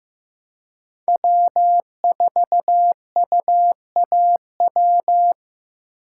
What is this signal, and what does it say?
Morse code, heard as a keyed tone, about 15 wpm, 700 Hz. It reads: W4UAW
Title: USA Call Signs - 5 Characters - 4x Character Spacing - 15wpm